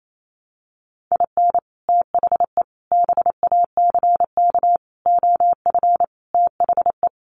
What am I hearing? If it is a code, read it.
Morse code, heard as a keyed tone, about 28 wpm, 700 Hz. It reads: IN THE BACK OF THE
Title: Sets of 5 Words 28wpm